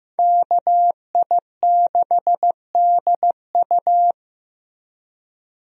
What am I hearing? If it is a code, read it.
Morse code, heard as a keyed tone, about 15 wpm, 700 Hz. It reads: KI6DU
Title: USA Call Signs - 5 Characters - Repeated 2x - 15wpm